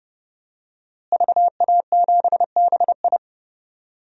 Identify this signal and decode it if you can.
Morse code, heard as a keyed tone, about 30 wpm, 700 Hz. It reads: VA7BI
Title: DXCC-100 Call Signs - 5 Characters - Repeated 2x - 30wpm